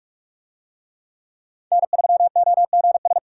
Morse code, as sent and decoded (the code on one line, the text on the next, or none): -. ...-- --- --. ...
N3OGS